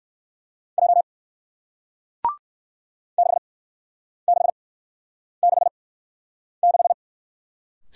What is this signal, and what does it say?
Morse code, heard as a keyed tone, about 58 wpm, spaced as for 9 wpm, 700 Hz. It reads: J 6666